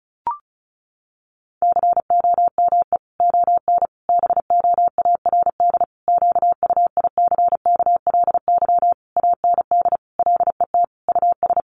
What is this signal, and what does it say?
Morse code, heard as a keyed tone, about 34 wpm, 700 Hz. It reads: COME ON BOARD QUICKLY AND LET US
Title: Sentences from Top 1300 Words 35wpm